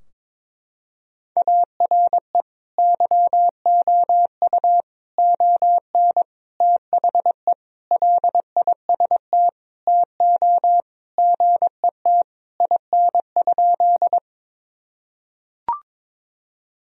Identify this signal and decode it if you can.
Morse code, heard as a keyed tone, about 22 wpm, 700 Hz. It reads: ARE YOU ON THE LIST TO GET IN?
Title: Sentences from Top 1000 Words - Review Part 2 - 22wpm